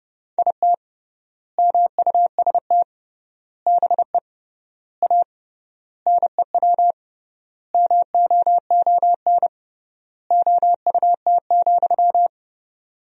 Morse code, heard as a keyed tone, about 30 wpm, 700 Hz. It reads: IT MUST BE A NEW MOON OUT,
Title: 3x Word-Spacing - Sentences from Top 500 Words - 30wpm